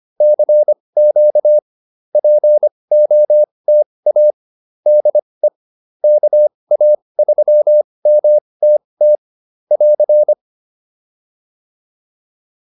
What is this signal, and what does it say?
Morse code, heard as a keyed tone, about 25 wpm, spaced as for 20 wpm, 600 Hz. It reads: CQ POTA DE KA3MTT <AR>